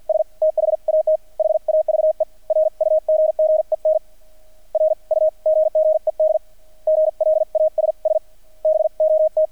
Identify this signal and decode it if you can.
Morse code, heard as a keyed tone, about 50 wpm, spaced as for 42 wpm, 630 Hz. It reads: S THAT HAVE RUGGEN RUGGED GLASS BOT
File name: archived / CRSnet